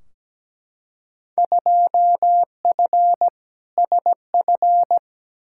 Morse code, heard as a keyed tone, about 17 wpm, 700 Hz. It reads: 2F SF
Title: Field Day Exchanges - 17wpm